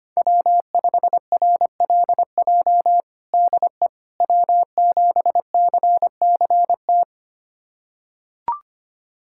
Morse code, heard as a keyed tone, about 25 wpm, 700 Hz. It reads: W5RLJ DE W7CCT
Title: USA Callsign DE Callsign - 5 Characters - 25wpm